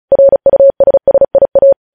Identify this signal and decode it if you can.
Morse code, heard as a keyed tone, about 36 wpm, 550 Hz. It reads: RUSSIA
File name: podcast / News headlines-1x-35x35